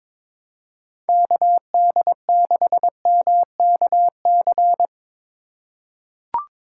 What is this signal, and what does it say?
Morse code, heard as a keyed tone, about 22 wpm, 700 Hz. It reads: KD6MKC
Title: USA Call Signs - 6 Characters - 2x Character Spacing - 22wpm